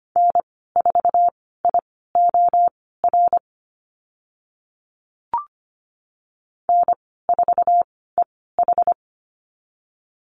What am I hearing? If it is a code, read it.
Morse code, heard as a keyed tone, about 25 wpm, spaced as for 16 wpm, 700 Hz. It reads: N4IOR N4EH
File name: Lesson 15g - Number 5 - Call Signs 25wpm